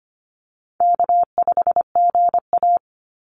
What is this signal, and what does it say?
Morse code, heard as a keyed tone, about 25 wpm, 700 Hz. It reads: K5GA
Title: Call Signs - 25wpm